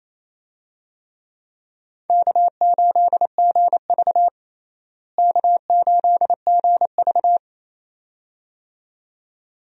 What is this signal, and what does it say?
Morse code, heard as a keyed tone, about 28 wpm, 700 Hz. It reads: K8GV K8GV
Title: USA Call Signs - 4 Characters - Repeated 2x - 28wpm